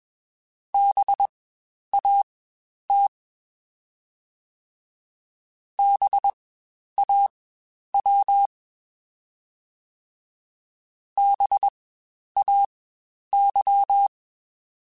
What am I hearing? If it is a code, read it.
Morse code, heard as a keyed tone, about 21 wpm, spaced as for 10 wpm, 800 Hz. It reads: BAT BAW BAY
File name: Spaced Alphas / Spaced Alphas mp3s / Bs sp @20